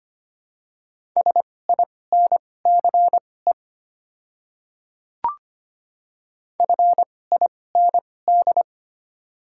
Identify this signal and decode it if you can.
Morse code, heard as a keyed tone, about 25 wpm, spaced as for 18 wpm, 700 Hz. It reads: SINCE FIND
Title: 2x Character Spacing - Top 500 Words - 25wpm